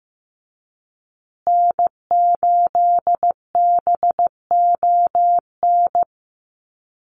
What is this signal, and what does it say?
Morse code, heard as a keyed tone, about 15 wpm, 700 Hz. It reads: N8BON
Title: USA Call Signs Repeated 2x 15wpm